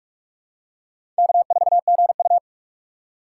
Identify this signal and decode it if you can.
Morse code, heard as a keyed tone, about 46 wpm, 700 Hz. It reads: K4GU
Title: USA Call Signs - 4 Characters - 3x Character Spacing - 45wpm